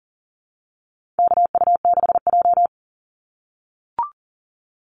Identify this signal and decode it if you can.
Morse code, heard as a keyed tone, about 40 wpm, 700 Hz. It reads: KU6J
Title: NA SOTA Activator Callsigns 40wpm